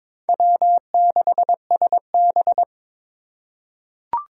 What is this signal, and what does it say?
Morse code, heard as a keyed tone, about 22 wpm, 700 Hz. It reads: W6SB
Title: USA Call Signs - 3x Character Spacing - 22wpm